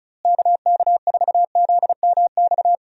Morse code, heard as a keyed tone, about 34 wpm, 700 Hz. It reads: KK4ZMX
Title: USA Call Signs - 6 Characters - Repeated 2x - 35wpm